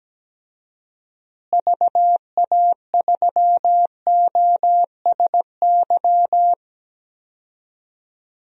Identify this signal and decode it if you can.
Morse code, heard as a keyed tone, about 17 wpm, 700 Hz. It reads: VA3OSY
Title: DXCC-100 Call Signs - 6 Characters - Repeated 2x - 17wpm